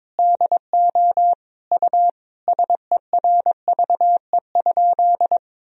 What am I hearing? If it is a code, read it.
Morse code, heard as a keyed tone, about 21 wpm, 700 Hz. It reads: DO U SERVE?
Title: Sentences from Top 400 Words and Abbreviations 22wpm